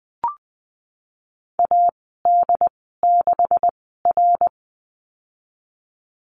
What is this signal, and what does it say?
Morse code, heard as a keyed tone, about 20 wpm, spaced as for 15 wpm, 700 Hz. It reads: AD6R
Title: 147 Lesson 24g - Letter y - Call Signs 20wpm